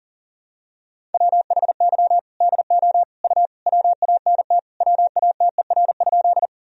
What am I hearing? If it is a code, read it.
Morse code, heard as a keyed tone, about 40 wpm, 700 Hz. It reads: WHY DO U WANT WATER?
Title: Sentences from Top 200 Words and Abbreviations 40wpm